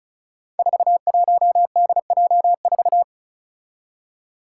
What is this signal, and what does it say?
Morse code, heard as a keyed tone, about 36 wpm, 700 Hz. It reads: 41DJ4
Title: Five Challenging Letter-Number Combinations 35wpm